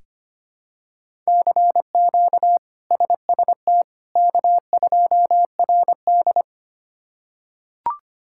Morse code, heard as a keyed tone, about 25 wpm, 700 Hz. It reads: CQ SST K2RD
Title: K1USN SST QSO Practice - 25wpm